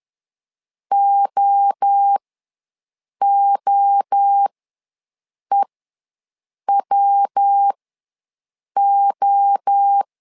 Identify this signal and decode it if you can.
Morse code, heard as a keyed tone, about 10 wpm, spaced as for 6 wpm, 800 Hz. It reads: OOEWO
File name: LEZIONI / 3 MWRO / 3 MWRO ws 10wpm 2310